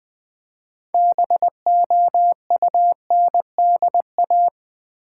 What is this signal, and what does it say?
Morse code, heard as a keyed tone, about 20 wpm, 700 Hz. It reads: BOUNDA
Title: Build A Word - 2 Letters At A Time - 8-Letter Words - 20wpm